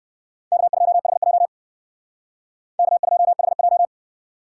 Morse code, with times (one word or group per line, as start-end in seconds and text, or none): 0.5-1.5 K2HJ
2.8-3.9 K2HJ